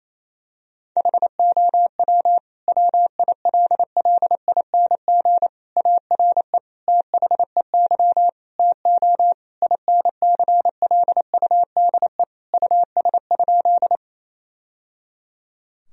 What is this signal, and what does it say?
Morse code, heard as a keyed tone, about 28 wpm, 700 Hz. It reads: HOW WILLING ARE THEY TO INCLUDE US?
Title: Sentences from Top 1000 Words - Review Part 3 - 28wpm